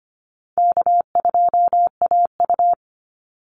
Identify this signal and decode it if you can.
Morse code, heard as a keyed tone, about 25 wpm, 700 Hz. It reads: K2AU
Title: Call Signs Repeated 3x - 25wpm